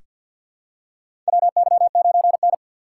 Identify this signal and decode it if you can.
Morse code, heard as a keyed tone, about 47 wpm, 700 Hz. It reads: WY9N
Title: USA Call Signs - 4 Characters - 6x Character Spacing - 50wpm